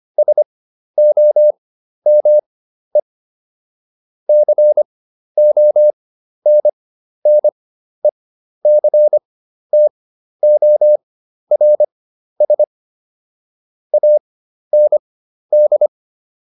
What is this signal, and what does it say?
Morse code, heard as a keyed tone, about 25 wpm, spaced as for 12 wpm, 600 Hz. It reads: SOME CONNECTORS AND